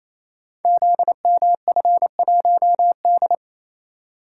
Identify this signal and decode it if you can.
Morse code, heard as a keyed tone, about 28 wpm, 700 Hz. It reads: ZMF1D
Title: Code Groups - 5 Letter:Numbers - 28wpm